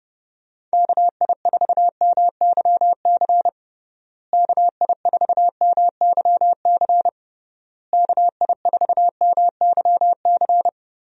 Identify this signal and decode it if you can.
Morse code, heard as a keyed tone, about 30 wpm, 700 Hz. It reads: KI4MYC KI4MYC KI4MYC
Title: USA Call Signs - 6 Characters - Repeated 3x - 30wpm